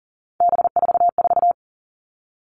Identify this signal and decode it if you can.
Morse code, heard as a keyed tone, about 40 wpm, 700 Hz. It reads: B44
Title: All Cut Numbers - 3-Digit 40wpm